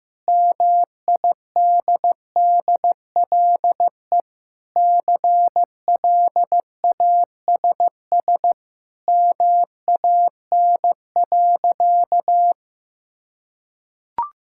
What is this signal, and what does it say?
Morse code, heard as a keyed tone, about 16 wpm, 700 Hz. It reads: MIDDLE CLASS MAN.
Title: Sentences from Top 1000 Words - Review Part 6 - 15wpm